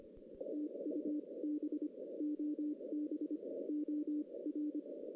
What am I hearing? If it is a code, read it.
Morse code, heard as a keyed tone, about 25 wpm, spaced as for 21 wpm, 300 Hz. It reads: TUB OB OR
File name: 2016 JAN CW NAQP / 2016 Jan naqp k5ot - 0-8